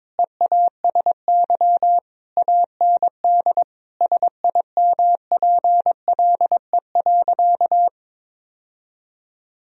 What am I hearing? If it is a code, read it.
Morse code, heard as a keyed tone, about 22 wpm, 700 Hz. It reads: EASY AND SIMPLE.
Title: Sentences from Top 1000 Words - Review Part 3 - 22wpm